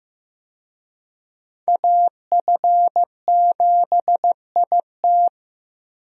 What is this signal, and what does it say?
Morse code, heard as a keyed tone, about 15 wpm, 700 Hz. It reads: AF7IT
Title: USA Call Signs - 5 Characters - Repeated 3x - 15wpm